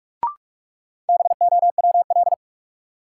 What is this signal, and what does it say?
Morse code, heard as a keyed tone, about 44 wpm, 700 Hz. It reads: BOWL
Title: Sentences from Top 1200 Words - New Word Drill Alt - 45wpm